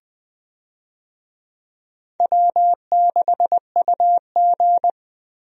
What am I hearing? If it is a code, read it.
Morse code, heard as a keyed tone, about 20 wpm, 700 Hz. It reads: W6UG
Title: USA Call Signs - 4 Characters - 20wpm